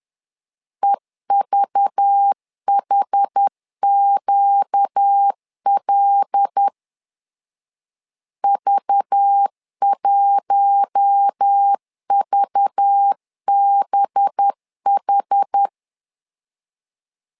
Morse code, heard as a keyed tone, about 10 wpm, 800 Hz. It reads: EVHQL V1VBH
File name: LEZIONI / 5 LBVC / 5 LBV 10wpm 2248